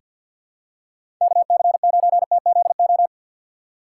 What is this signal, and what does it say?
Morse code, heard as a keyed tone, about 48 wpm, 700 Hz. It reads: KC9TZO
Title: K1USN SST QSO Practice - 50wpm